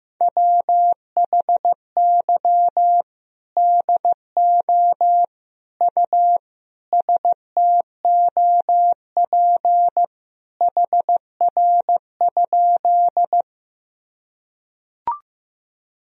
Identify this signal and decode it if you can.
Morse code, heard as a keyed tone, about 15 wpm, 700 Hz. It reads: WHY DO U STOP HR?
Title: Sentences from Top 300 Words and Abbreviations 15wpm